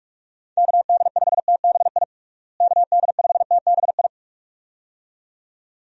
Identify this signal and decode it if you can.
Morse code, heard as a keyed tone, about 46 wpm, 700 Hz. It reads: KD5TBI KD5TBI
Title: USA Call Signs - 6 Characters - Repeated 2x - 45wpm